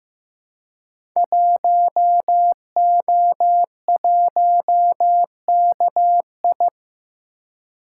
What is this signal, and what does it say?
Morse code, heard as a keyed tone, about 15 wpm, 700 Hz. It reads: 1O1KI
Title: Code Groups - 5 Letter:Numbers - 15wpm